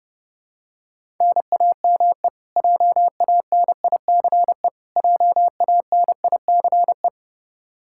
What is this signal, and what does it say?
Morse code, heard as a keyed tone, about 30 wpm, 700 Hz. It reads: NAME JANICE JANICE
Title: Practice QSO - Exchange 1 - Beginning - 30wpm